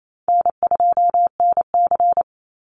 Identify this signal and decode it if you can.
Morse code, heard as a keyed tone, about 28 wpm, 700 Hz. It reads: N2NC
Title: Easy Call Signs Repeated 2x - 28wpm